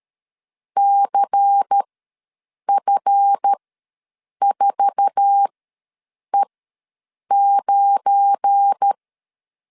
Morse code, heard as a keyed tone, about 13 wpm, spaced as for 7 wpm, 800 Hz. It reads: CF4E9
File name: LEZIONI / 8 YJ45 / 8 YJ4 ws 12wpm 3249